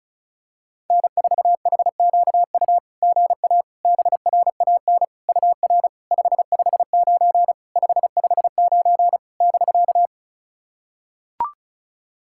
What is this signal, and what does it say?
Morse code, heard as a keyed tone, about 36 wpm, 700 Hz. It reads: N4HQU GA BRAN UR 559 559 <BK>
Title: SOTA QSO Practice 35wpm